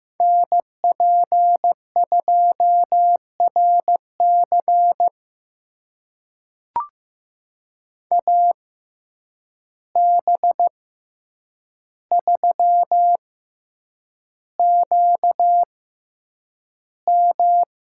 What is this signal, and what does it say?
Morse code, heard as a keyed tone, about 15 wpm, spaced as for 8 wpm, 700 Hz. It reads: NP2RC AB3QM
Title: USA Call Signs - 5 Characters - 6x Character Spacing - 15wpm